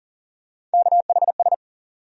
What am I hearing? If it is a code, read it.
Morse code, heard as a keyed tone, about 40 wpm, 700 Hz. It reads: KHS
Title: USA Call Sign Postfixes - 3 Letters 40wpm